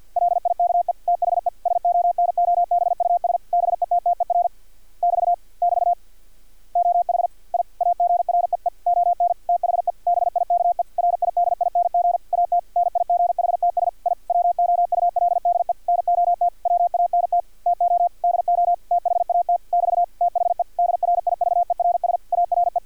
Morse code, heard as a keyed tone, about 50 wpm, 700 Hz. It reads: LIKE THE SONOBUS BETTER <BT> <BT> OH I AGREE ON THE BIKE RIDING AT NIGHTS I WOULDE NOT WANT TO DO THAT <BT> THE DRIVERS ARE
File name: archived / CRSnet